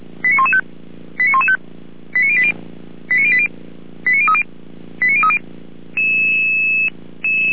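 Five-tone selective call (ZVEI), heard as quick runs of five tones, tones of about 70 ms. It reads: 89115 89115 89080 89080 89030 89030
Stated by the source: ZVEI five-tone selective call received by SDR.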